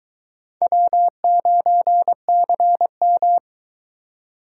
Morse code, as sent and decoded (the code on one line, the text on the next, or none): .-- ----. -.-. --
W9CM